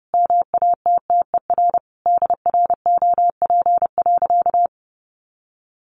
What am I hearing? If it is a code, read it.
Morse code, heard as a keyed tone, about 30 wpm, 700 Hz. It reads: MATTER DROP.
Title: Sentences from Top 1400 Words 30wpm